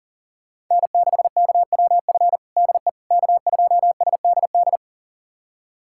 Morse code, heard as a keyed tone, about 40 wpm, 690 Hz. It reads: N6KWF DE K2SDD
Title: USA Callsign DE Callsign - 5 Characters - 40wpm